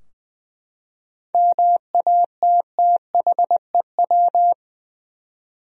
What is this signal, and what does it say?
Morse code, heard as a keyed tone, about 20 wpm, 700 Hz. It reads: MATTHEW
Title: USA Common Names - 20wpm